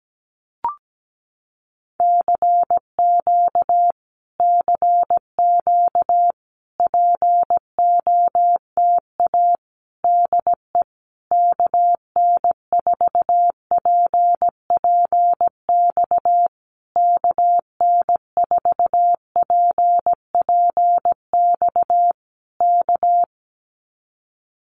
Morse code, heard as a keyed tone, about 17 wpm, 700 Hz. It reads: CQ CQ POTA DE KN4PPX KN4PPX K
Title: Common QSO Phrases Encore 17wpm